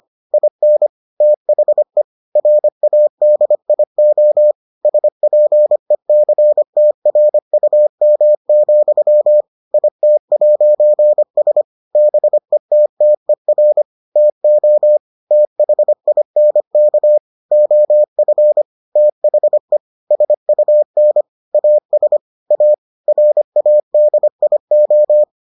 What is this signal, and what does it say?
Morse code, heard as a keyed tone, about 25 wpm, 600 Hz. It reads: IN THE RADIO SPECTRUM, IT'S BETTER TO THINK OF THE SUN AS A RADIO